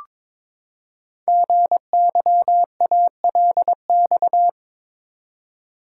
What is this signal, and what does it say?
Morse code, heard as a keyed tone, about 22 wpm, 700 Hz. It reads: GYALX
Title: Code Groups - 5 Letters - 22wpm